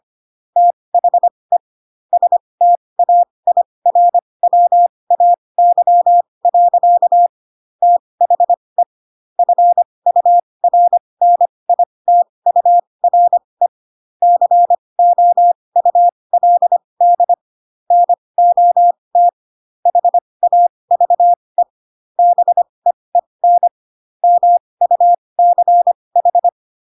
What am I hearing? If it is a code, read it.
Morse code, heard as a keyed tone, about 25 wpm, spaced as for 20 wpm, 700 Hz. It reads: THE STAIRWAY. THE FURNITURE COULD NOT HAVE BEEN MUCH
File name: Pinocchio / Pinocchio ch3 15 25 ik1hge